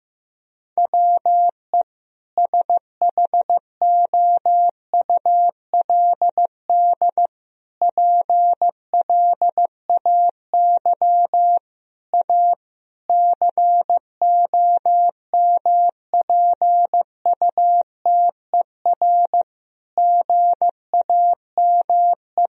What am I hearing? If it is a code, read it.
Morse code, heard as a keyed tone, about 15 wpm, 700 Hz. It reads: WE SHOULD PLAY A COMPUTER GAME
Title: Sentences from Top 1000 Words - Review Part 3 - 15wpm